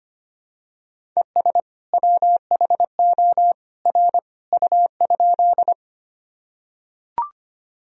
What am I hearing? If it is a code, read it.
Morse code, heard as a keyed tone, about 25 wpm, 700 Hz. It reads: ES WHO R U?
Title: Sentences from Top 100 Words and Abbreviations 25wpm